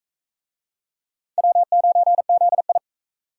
Morse code, heard as a keyed tone, about 42 wpm, 700 Hz. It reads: W9ZI